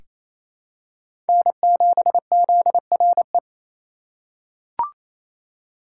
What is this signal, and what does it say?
Morse code, heard as a keyed tone, about 27 wpm, 700 Hz. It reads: N7ZRE
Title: USA Call Signs - 5 Characters - 4x Character Spacing - 28wpm